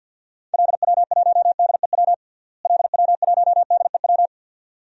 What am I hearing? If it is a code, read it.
Morse code, heard as a keyed tone, about 53 wpm, 700 Hz. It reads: LW1DEW LW1DEW
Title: DXCC-100 Call Signs - 6 Characters - Repeated 2x - 50wpm